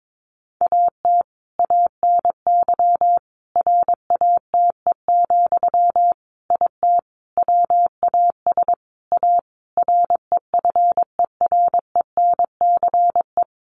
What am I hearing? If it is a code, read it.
Morse code, heard as a keyed tone, about 22 wpm, 700 Hz. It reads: AT ANY RATE, IT WAS A REFERENCE
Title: Sentences from Top 1200 Words 22wpm